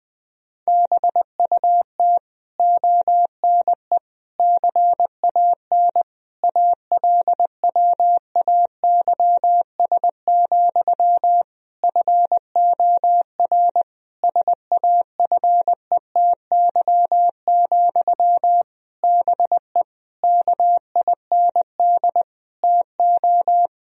Morse code, heard as a keyed tone, about 21 wpm, 700 Hz. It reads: BUT ONE CAN ALWAYS, FOR SAFETY, BE KIND TO
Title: Sentences from Top 1000 Words - Review Part 4 - 20wpm